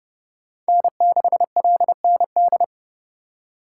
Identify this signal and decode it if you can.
Morse code, heard as a keyed tone, about 29 wpm, 700 Hz. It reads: N6LND